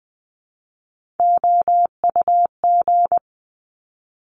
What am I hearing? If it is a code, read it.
Morse code, heard as a keyed tone, about 20 wpm, 700 Hz. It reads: OUG
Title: Top Three Letter Combinations in Words 20wpm